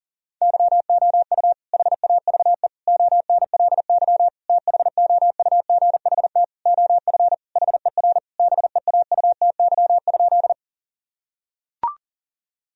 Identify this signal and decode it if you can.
Morse code, heard as a keyed tone, about 40 wpm, 700 Hz. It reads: YOU HAVE ONLY THOUGHT OF HER BEAUTY?
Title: Sentences from Top 1000 Words - Review Part 2 - 40wpm